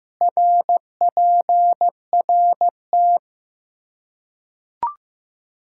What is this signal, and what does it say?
Morse code, heard as a keyed tone, about 15 wpm, 700 Hz. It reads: RPRT
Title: K1USN SST QSO Practice 15wpm